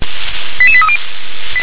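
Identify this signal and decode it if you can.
Five-tone selective call (ZVEI), heard as quick runs of five tones, tones of about 70 ms. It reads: 88830
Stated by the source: ZVEI five-tone selective call received by SDR.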